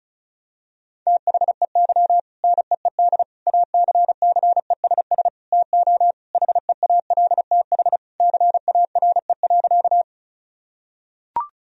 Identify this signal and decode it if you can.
Morse code, heard as a keyed tone, about 36 wpm, 700 Hz. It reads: THEY NEED ACCESS TO HEALTH CARE.
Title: Sentences from Top 1000 Words - Review Part 4 - 35wpm